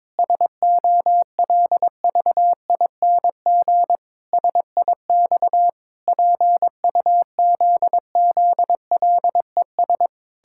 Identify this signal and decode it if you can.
Morse code, heard as a keyed tone, about 22 wpm, 700 Hz. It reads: SOLVING SIX PUZZLES
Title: Pangram Challenge v2 - 22wpm